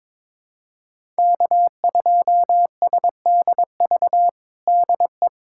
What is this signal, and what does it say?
Morse code, heard as a keyed tone, about 22 wpm, 700 Hz. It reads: K2SDV DE
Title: USA Callsign DE Callsign - 5 Characters - 22wpm